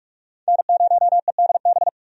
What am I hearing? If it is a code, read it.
Morse code, heard as a keyed tone, about 45 wpm, 700 Hz. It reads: N0EDB
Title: USA Call Signs - 5 Characters - 45wpm